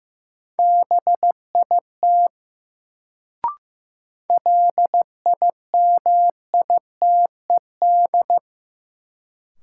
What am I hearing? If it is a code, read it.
Morse code, heard as a keyed tone, about 15 wpm, 700 Hz. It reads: BIT LIMITED
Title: Sentences from Top 900 Words - New Word Drill - 15wpm